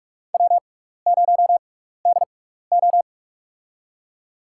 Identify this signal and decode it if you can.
Morse code, heard as a keyed tone, about 46 wpm, spaced as for 15 wpm, 700 Hz. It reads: W0DO
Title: USA Call Signs - 4 Characters - 6x Character Spacing - 45wpm